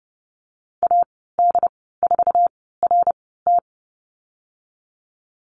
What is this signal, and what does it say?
Morse code, heard as a keyed tone, about 32 wpm, spaced as for 17 wpm, 700 Hz. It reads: AD4RT
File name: Lesson 14g - Number 2 - Call Signs 30wpm